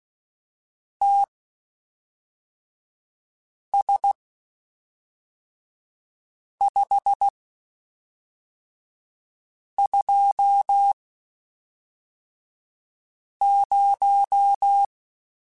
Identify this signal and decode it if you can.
Morse code, heard as a keyed tone, about 16 wpm, spaced as for 3 wpm, 800 Hz. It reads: TS520
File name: (spaced) / QSOs Spaced-mp3s / WD0FBT 1 @5 sp